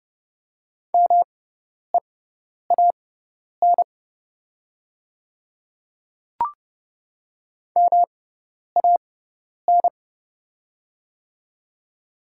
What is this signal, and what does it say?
Morse code, heard as a keyed tone, about 30 wpm, spaced as for 10 wpm, 700 Hz. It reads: MEAN MAN
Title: 6x Character Spacing - Top 200 Words - 30wpm